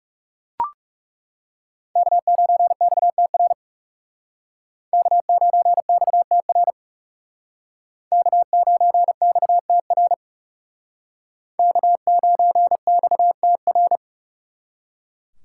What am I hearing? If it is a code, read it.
Morse code, heard as a keyed tone, about 36 wpm, 700 Hz. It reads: K9XTR K9XTR K9XTR K9XTR
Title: Speed Racing - USA 1x3 Call Signs 30wpm